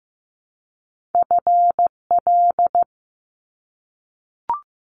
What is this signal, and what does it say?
Morse code, heard as a keyed tone, about 15 wpm, 700 Hz. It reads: FL
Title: Top Two Letter Combinations in Words 15wpm